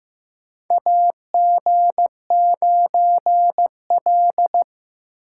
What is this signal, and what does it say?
Morse code, heard as a keyed tone, about 15 wpm, 700 Hz. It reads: AG9L
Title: USA Call Signs - 4 Characters - 4x Character Spacing - 15wpm